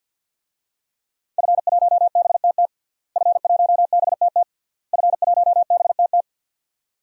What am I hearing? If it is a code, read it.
Morse code, heard as a keyed tone, about 46 wpm, 700 Hz. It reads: F1BTT F1BTT F1BTT
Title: DXCC-100 Call Signs - 5 Characters - Repeated 3x - 50wpm